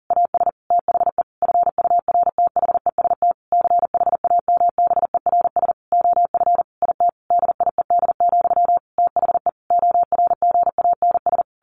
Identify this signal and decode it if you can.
Morse code, heard as a keyed tone, about 40 wpm, 700 Hz. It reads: AS THE FURTHEST CHAMBERS OF IT DIED, THE ORGANS
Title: A Sound Of Thunder 40wpm